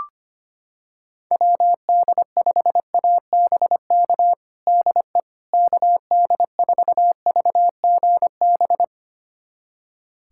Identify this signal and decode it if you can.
Morse code, heard as a keyed tone, about 25 wpm, 700 Hz. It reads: WD5ABK DE KD4VGB
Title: USA Callsign DE Callsign 25wpm